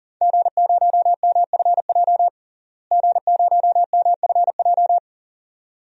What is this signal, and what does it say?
Morse code, heard as a keyed tone, about 40 wpm, 700 Hz. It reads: G0MFJ G0MFJ
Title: DXCC-100 Call Signs - 5 Characters - Repeated 2x - 40wpm